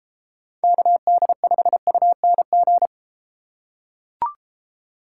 Morse code, heard as a keyed tone, about 32 wpm, 700 Hz. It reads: KD5UNG